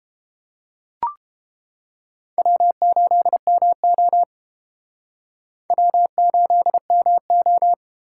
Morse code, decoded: W8MO W8MO

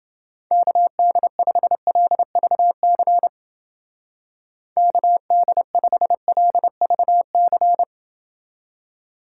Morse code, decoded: KD5LVC KD5LVC

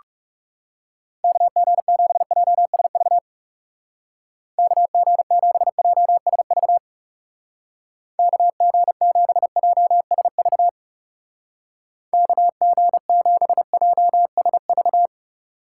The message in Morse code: KG7JSV KG7JSV KG7JSV KG7JSV

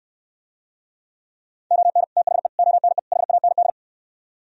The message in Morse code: ON THE ONE HAND

